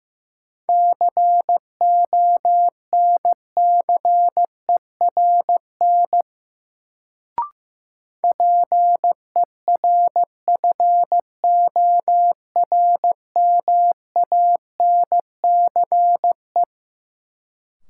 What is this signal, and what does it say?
Morse code, heard as a keyed tone, about 15 wpm, 700 Hz. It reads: CONCERN PERFORMANCE
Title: Sentences from Top 1000 Words - New Word Drill - 15wpm